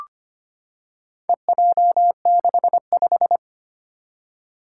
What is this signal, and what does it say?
Morse code, heard as a keyed tone, about 25 wpm, 700 Hz. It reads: EJ65
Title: Four Challenging Letter-Number Combinations 25wpm